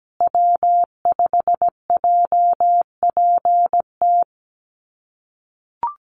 Morse code, heard as a keyed tone, about 17 wpm, 700 Hz. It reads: W5JPT
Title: USA Call Signs Repeated 2x 17wpm